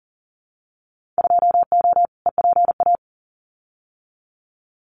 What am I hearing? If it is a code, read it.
Morse code, heard as a keyed tone, about 40 wpm, 700 Hz. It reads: 2O EPA
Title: Winter Field Day Exchanges Encore 40wpm